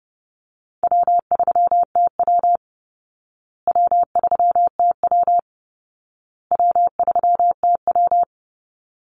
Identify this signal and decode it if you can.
Morse code, heard as a keyed tone, about 28 wpm, 700 Hz. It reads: W3TW W3TW W3TW
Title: Easy Call Signs Repeated 3x - 30wpm